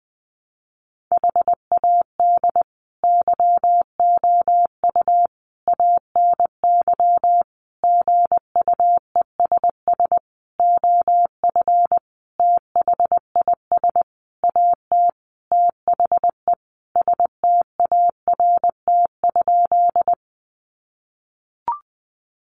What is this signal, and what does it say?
Morse code, heard as a keyed tone, about 20 wpm, 700 Hz. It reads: HAD YOU ANY GUESS OF THIS AT THE START?
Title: Sentences from Top 1500 Words 20wpm